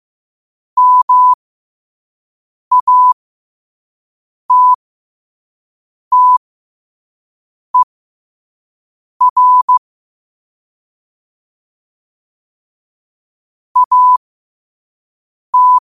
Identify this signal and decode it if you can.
Morse code, heard as a keyed tone, about 15 wpm, spaced as for 5 wpm, 1000 Hz. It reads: MATTER AT
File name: qotdfiles / 5wpm